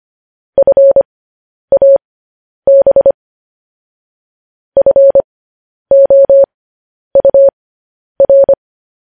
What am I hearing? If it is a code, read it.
Morse code, heard as a keyed tone, about 25 wpm, spaced as for 10 wpm, 550 Hz. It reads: FAB FOUR
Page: news headlines in morse code at 10 wpm